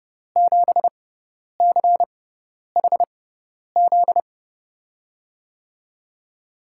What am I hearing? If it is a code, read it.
Morse code, heard as a keyed tone, about 30 wpm, spaced as for 10 wpm, 700 Hz. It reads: 7CHZ
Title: USA Call Signs - 5 Characters - 6x Character Spacing - 30wpm